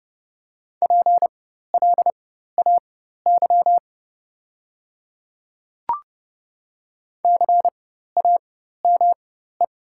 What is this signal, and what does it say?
Morse code, heard as a keyed tone, about 30 wpm, spaced as for 14 wpm, 700 Hz. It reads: PLAY CAME
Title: 4x Character Spacing - Top 300 Words - 30wpm